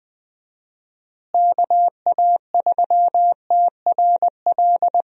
Morse code, heard as a keyed tone, about 21 wpm, 700 Hz. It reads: KA3TRL
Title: USA Call Signs - 6 Characters - Repeated 3x - 20wpm